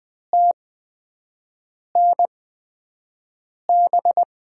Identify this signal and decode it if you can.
Morse code, heard as a keyed tone, about 20 wpm, spaced as for 5 wpm, 700 Hz. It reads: TNB
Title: USA Call Signs - 8x Character Spacing - 20wpm